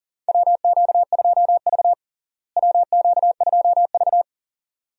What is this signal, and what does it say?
Morse code, heard as a keyed tone, about 40 wpm, 700 Hz. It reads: WQ2V WQ2V
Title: USA Call Signs - 4 Characters - Repeated 2x - 40wpm